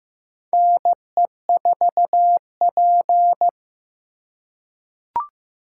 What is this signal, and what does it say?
Morse code, heard as a keyed tone, about 15 wpm, 700 Hz. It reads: NE4P
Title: USA Call Signs - 4 Characters - 2x Character Spacing - 15wpm